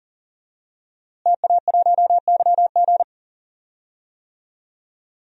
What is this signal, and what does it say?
Morse code, heard as a keyed tone, about 40 wpm, 700 Hz. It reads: TA1YG
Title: DXCC-100 Call Signs - 5 Characters - Repeated 3x - 40wpm